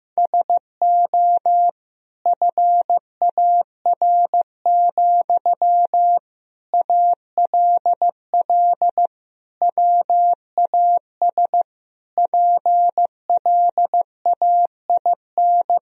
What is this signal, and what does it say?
Morse code, heard as a keyed tone, about 15 wpm, 700 Hz. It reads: SO FAR, ALL WAS PLAIN
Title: Sentences from Top 1000 Words - Review Part 4 - 15wpm